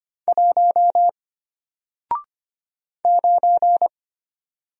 Morse code, heard as a keyed tone, about 25 wpm, 700 Hz. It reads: 1 9
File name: Single-Letter-Number-25wpm